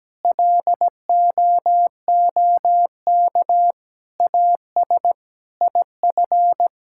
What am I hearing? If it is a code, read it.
Morse code, heard as a keyed tone, about 17 wpm, 700 Hz. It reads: LOOK AS IF
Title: Sentences from Top 1000 Words - Review Part 6 - 17wpm